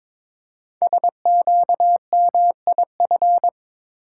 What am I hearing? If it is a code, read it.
Morse code, heard as a keyed tone, about 22 wpm, 700 Hz. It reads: SQMIF
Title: Code Groups - 5 Letters - 22wpm